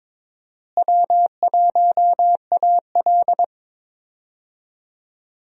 Morse code, heard as a keyed tone, about 22 wpm, 700 Hz. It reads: W1AL
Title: USA Call Signs - 4 Characters - Repeated 2x - 22wpm